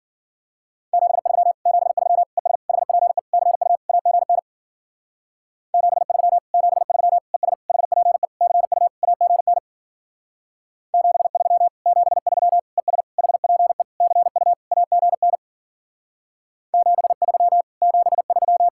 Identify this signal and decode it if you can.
Morse code, heard as a keyed tone, about 50 wpm, 700 Hz. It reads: 73 73 ES HPE CU AGN 73 73 ES HPE CU AGN 73 73 ES HPE CU AGN 73 73